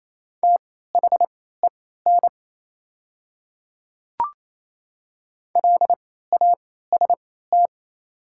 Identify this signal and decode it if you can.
Morse code, heard as a keyed tone, about 28 wpm, spaced as for 16 wpm, 700 Hz. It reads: THEN LAST
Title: 3x Character Spacing - Top 300 Words - 28wpm